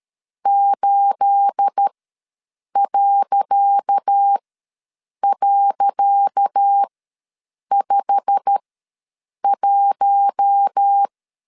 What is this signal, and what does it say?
Morse code, heard as a keyed tone, about 13 wpm, spaced as for 7 wpm, 800 Hz. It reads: 8..51
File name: LEZIONI / 10 SIGNS / 10C DOT ws 12wpm 3548